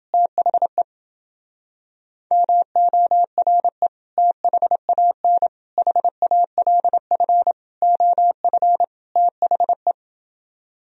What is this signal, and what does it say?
Morse code, heard as a keyed tone, about 30 wpm, 700 Hz. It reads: THE MORE THAN HALF OF THE